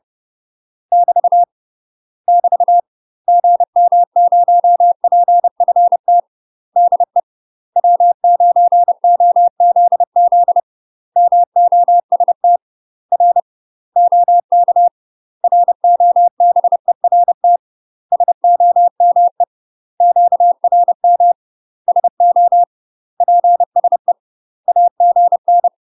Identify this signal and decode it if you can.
Morse code, heard as a keyed tone, about 30 wpm, 700 Hz. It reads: <BT> <BT> GM0PFT DE W9OZZ MOST R OK ROBERT SOME QRM SO PSE AGN
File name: UK MASTER QSOs TRAINING / 30WPM / UK MASTER QSOs 0004